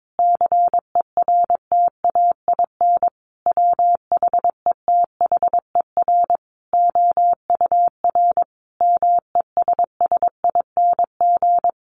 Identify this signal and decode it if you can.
Morse code, heard as a keyed tone, about 22 wpm, 700 Hz. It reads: CERTAIN WHETHER OUR MESSING
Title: A Sound Of Thunder 22wpm